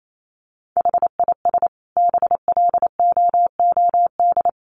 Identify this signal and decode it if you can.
Morse code, heard as a keyed tone, about 28 wpm, 700 Hz. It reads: HIS BLOOD
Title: Sentences from Top 1100 Words 28wpm